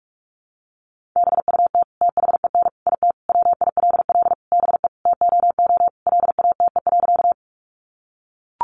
Morse code, heard as a keyed tone, about 44 wpm, 700 Hz. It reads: BUT THEN IT WILL BE TOO LATE.
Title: Sentences from Top 300 Words 45wpm